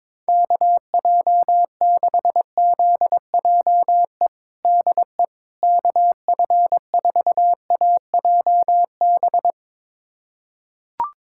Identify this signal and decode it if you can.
Morse code, heard as a keyed tone, about 22 wpm, 700 Hz. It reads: KJ6ZJE DE KF4AJB
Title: USA Callsign DE Callsign 22wpm